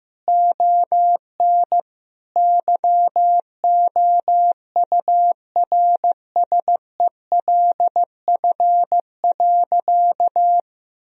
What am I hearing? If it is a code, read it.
Morse code, heard as a keyed tone, about 14 wpm, 700 Hz. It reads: ON YOURSELF.
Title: Sentences from Top 1000 Words - Review Part 2 - 15wpm